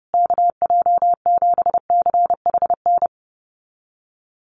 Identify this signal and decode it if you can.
Morse code, heard as a keyed tone, about 30 wpm, 700 Hz. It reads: KJ7CHN
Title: USA Call Signs Repeated 3x 30wpm